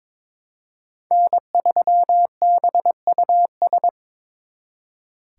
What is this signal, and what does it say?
Morse code, heard as a keyed tone, about 26 wpm, 700 Hz. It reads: N3BUS
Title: Speed Racing - USA Call Signs 22wpm